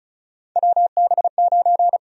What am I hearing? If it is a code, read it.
Morse code, heard as a keyed tone, about 36 wpm, 700 Hz. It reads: WB9
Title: USA Call Sign Prefixes - 35wpm